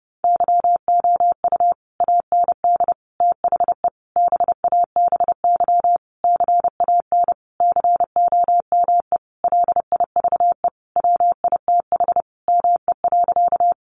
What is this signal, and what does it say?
Morse code, heard as a keyed tone, about 30 wpm, 700 Hz. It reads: YOU AND THE BABY CAN COME LIVE WITH ME.
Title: Sentences from Top 1300 Words 30wpm